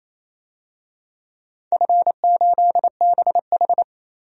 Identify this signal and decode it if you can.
Morse code, heard as a keyed tone, about 28 wpm, 700 Hz. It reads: F8BH
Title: DXCC-100 Call Signs - 4 Characters - Repeated 3x - 28wpm